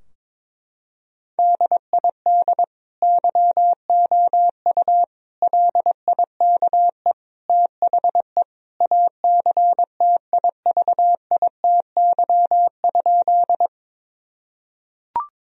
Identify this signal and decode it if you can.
Morse code, heard as a keyed tone, about 22 wpm, 700 Hz. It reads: DID YOU LIKE THE ACTIVITY?
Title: Sentences from Top 1000 Words - Review Part 4 - 22wpm